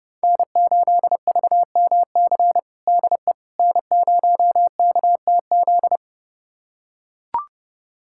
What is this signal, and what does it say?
Morse code, heard as a keyed tone, about 30 wpm, 700 Hz. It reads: N8VMC DE N0KTZ
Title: USA Callsign DE Callsign - 5 Characters - 30wpm